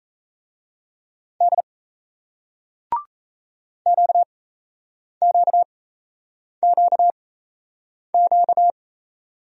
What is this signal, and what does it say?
Morse code, heard as a keyed tone, about 37 wpm, spaced as for 8 wpm, 700 Hz. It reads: D QQQQ